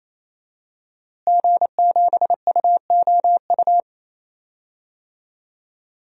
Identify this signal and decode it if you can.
Morse code, heard as a keyed tone, about 28 wpm, 700 Hz. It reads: G7UOU
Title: DXCC-100 Call Signs - 5 Characters - Repeated 3x - 28wpm